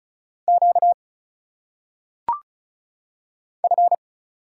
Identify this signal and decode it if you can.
Morse code, heard as a keyed tone, about 35 wpm, 700 Hz. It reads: Q F
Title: Single Letters - Phonetic Pronunciation 35wpm